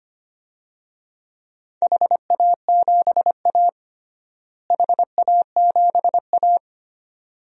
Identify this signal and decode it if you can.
Morse code, heard as a keyed tone, about 25 wpm, 700 Hz. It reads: HA7A HA7A
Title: DXCC-100 Call Signs - 4 Characters - Repeated 2x - 25wpm